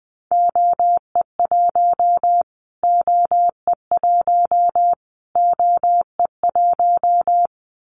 Morse code, heard as a keyed tone, about 20 wpm, 700 Hz. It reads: OE1 OE1 OE1
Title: Callsign Prefixes for Top 25 DXCC - Repeated 3x - 20wpm